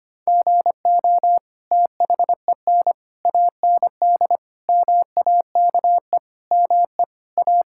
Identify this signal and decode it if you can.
Morse code, heard as a keyed tone, about 25 wpm, 700 Hz. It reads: GO THEN AND MAKE ME A
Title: Sentences from Top 1000 Words - Review Part 5 - 25wpm